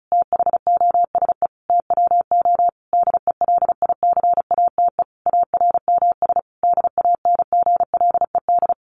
Morse code, heard as a keyed tone, about 34 wpm, 700 Hz. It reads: THOSE TWO DELICATE ARMS DANGLED
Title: A Sound Of Thunder 35wpm